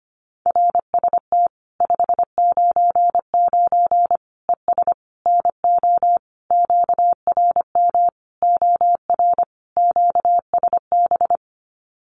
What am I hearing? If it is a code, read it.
Morse code, heard as a keyed tone, about 25 wpm, 700 Hz. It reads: RST 599 ES NO QRM OR QSB
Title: Common QSO Phrases 25wpm